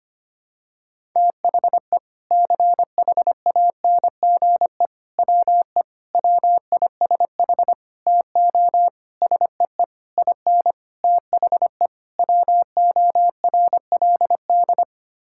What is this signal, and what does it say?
Morse code, heard as a keyed tone, about 25 wpm, 700 Hz. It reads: THE CHANGE WE WISH TO SEE IN THE WORLD
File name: Sets of 9 Words - 25wpm